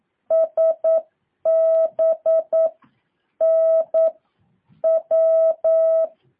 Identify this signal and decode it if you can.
Morse code, heard as a keyed tone, about 9 wpm, 650 Hz. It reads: S B N W
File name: jh8cbh / 2023cw